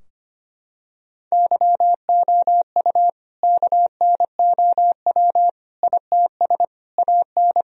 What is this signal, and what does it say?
Morse code, heard as a keyed tone, about 25 wpm, 700 Hz. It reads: YOU KNOW ITS AN
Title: Sentences from Top 1000 Words - Review Part 4 - 25wpm